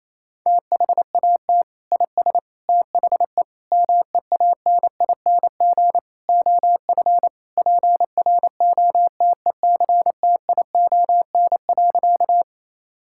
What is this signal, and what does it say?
Morse code, heard as a keyed tone, about 28 wpm, 700 Hz. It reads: THAT IS THE MEANING OF PROTECTION.
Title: Sentences from Top 1600 Words 28wpm